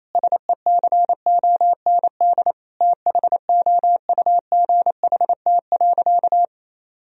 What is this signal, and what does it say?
Morse code, heard as a keyed tone, about 28 wpm, 700 Hz. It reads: SECOND THOUGHT.
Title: Sentences from Top 1000 Words - Review Part 1 - 28wpm